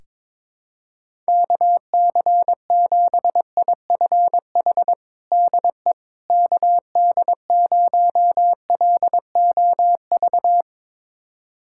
Morse code, heard as a keyed tone, about 21 wpm, 700 Hz. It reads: KC7IFH DE KD0LOV
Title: USA Callsign DE Callsign 22wpm